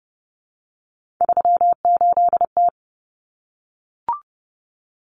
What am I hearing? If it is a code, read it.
Morse code, heard as a keyed tone, about 30 wpm, 700 Hz. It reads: 38T
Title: Common Cut Numbers - 3-Digit 30wpm